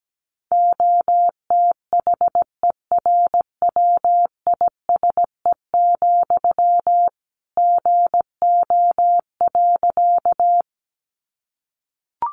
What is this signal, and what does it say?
Morse code, heard as a keyed tone, about 17 wpm, 700 Hz. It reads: OTHERWISE, GO.
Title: Sentences from Top 1200 Words 17wpm